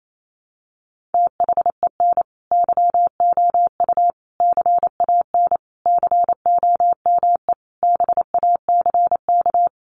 Morse code, heard as a keyed tone, about 28 wpm, 700 Hz. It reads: THEN YOU CAN COME BACK
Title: A Sound Of Thunder 28wpm